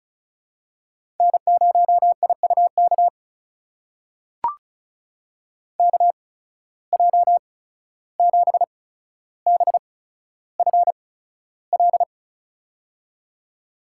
Morse code, heard as a keyed tone, about 36 wpm, spaced as for 14 wpm, 700 Hz. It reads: N0IUK KJ7BFL
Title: USA Call Signs - 8x Character Spacing - 35wpm